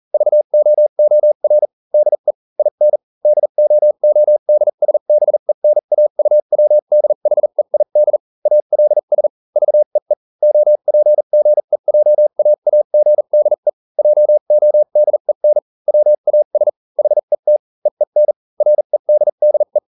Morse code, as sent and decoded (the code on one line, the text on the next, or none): ...- --- --- .-. / -.. / .. -. / -.. --- --- -.. ... -... -. .- ..- .-- -.. .... .. -.. / .- .-.. ... / ...- . / --- .--. --. .--- .- .- --. -.. / .--- --- -.. -. / .-- .- ... / .... - / . -. / .-. -.. -.. .
VOOR DE IN DOODSBENAUWDHEID ALS VEE OPGEJAAGDE JODEN WAS HET EEN REDDE